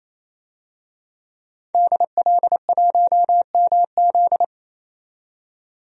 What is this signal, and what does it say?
Morse code, heard as a keyed tone, about 28 wpm, 700 Hz. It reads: DL1MZ
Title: DXCC-100 Call Signs - 5 Characters - 28wpm